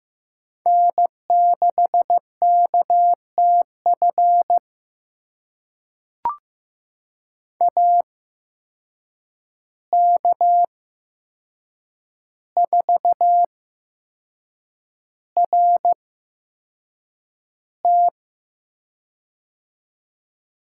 Morse code, heard as a keyed tone, about 15 wpm, spaced as for 6 wpm, 700 Hz. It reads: N6KTF AK4RT
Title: USA Call Signs - 5 Characters - 8x Character Spacing - 15wpm